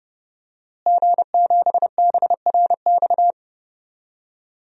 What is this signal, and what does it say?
Morse code, heard as a keyed tone, about 30 wpm, 700 Hz. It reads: G7BRX
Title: DXCC-100 Call Signs - 5 Characters - Repeated 2x - 30wpm